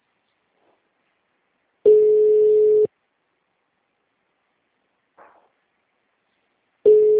Ringback tone (Europe, 425 Hz).